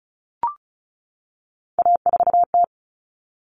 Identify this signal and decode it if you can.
Morse code, heard as a keyed tone, about 34 wpm, 700 Hz. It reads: A4T
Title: All Cut Numbers - 3-Digit 35wpm